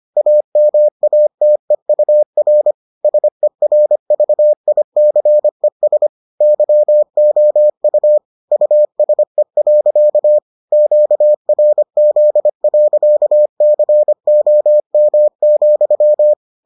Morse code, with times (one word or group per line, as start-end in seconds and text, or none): 0.2-2.7 AMATEUR
3.0-6.1 SERVICES
6.4-8.2 YOU
8.5-10.4 USE.
10.7-16.3 QRZ.COM,